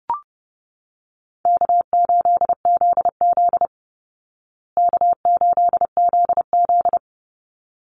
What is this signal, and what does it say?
Morse code, heard as a keyed tone, about 30 wpm, 700 Hz. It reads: K8ZZ K8ZZ
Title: Easy Call Signs Repeated 2x - 30wpm